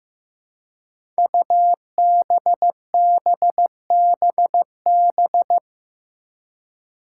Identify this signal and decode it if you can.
Morse code, heard as a keyed tone, about 15 wpm, 700 Hz. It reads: UBBBB
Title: Five Challenging Letter-Number Combinations 15wpm